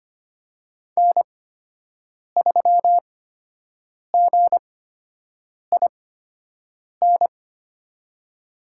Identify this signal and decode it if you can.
Morse code, heard as a keyed tone, about 25 wpm, spaced as for 7 wpm, 700 Hz. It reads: N3GIN
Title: USA Call Signs - 5 Characters - 8x Character Spacing - 25wpm